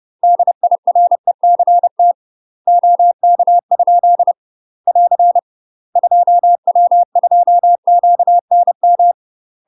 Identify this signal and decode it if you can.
Morse code, heard as a keyed tone, about 30 wpm, 700 Hz. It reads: DIRECT OK? <AR> 2W2QNM
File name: UK MASTER QSOs TRAINING / 30WPM / UK MASTER QSOs 0036